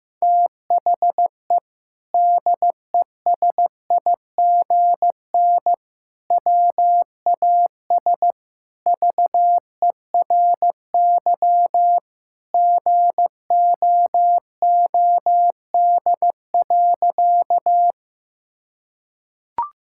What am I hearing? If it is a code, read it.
Morse code, heard as a keyed tone, about 15 wpm, 700 Hz. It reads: THE DESIGN WAS VERY GOOD.
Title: Sentences from Top 1000 Words - Review Part 5 - 15wpm